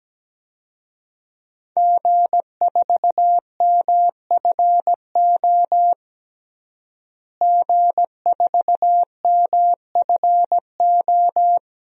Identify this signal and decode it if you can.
Morse code, heard as a keyed tone, about 17 wpm, 700 Hz. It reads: G4MFO G4MFO
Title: DXCC-100 Call Signs - 5 Characters - Repeated 2x - 17wpm